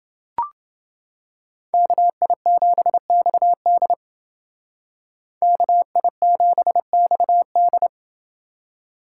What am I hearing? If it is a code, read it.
Morse code, heard as a keyed tone, about 28 wpm, 700 Hz. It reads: KI7XD KI7XD